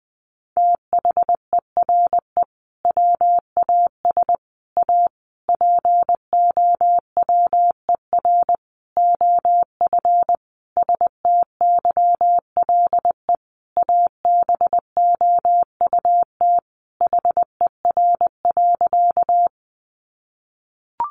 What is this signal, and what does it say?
Morse code, heard as a keyed tone, about 20 wpm, 700 Hz. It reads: THERE WAS A POWER OF STYLE ABOUT HER.
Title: Sentences from Top 800 Words 20wpm